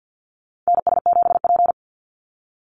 Morse code, heard as a keyed tone, about 49 wpm, 700 Hz. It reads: NS7P
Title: NA SOTA Chaser Callsigns 50wpm